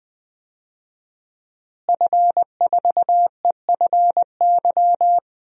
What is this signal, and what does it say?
Morse code, heard as a keyed tone, about 20 wpm, 700 Hz. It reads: F4EFY
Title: DXCC-100 Call Signs - 5 Characters - Repeated 3x - 20wpm